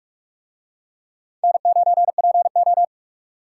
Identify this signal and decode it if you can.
Morse code, heard as a keyed tone, about 44 wpm, 700 Hz. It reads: N9PO